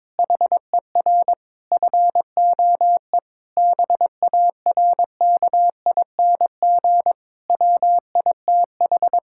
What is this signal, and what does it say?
Morse code, heard as a keyed tone, about 22 wpm, 700 Hz. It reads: HER FOE BARKING WITH
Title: Pangram Challenge v2 - 22wpm